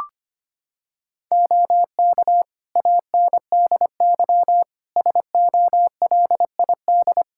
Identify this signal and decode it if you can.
Morse code, heard as a keyed tone, about 25 wpm, 700 Hz. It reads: OK ANDY SOLID
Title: K1USN SST QSO Practice 25wpm